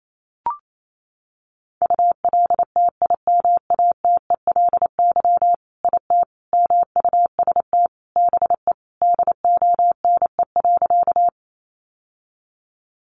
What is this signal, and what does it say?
Morse code, heard as a keyed tone, about 28 wpm, 700 Hz. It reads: ULTIMATELY IT MUST BE DONE.
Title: Sentences from Top 1200 Words 28wpm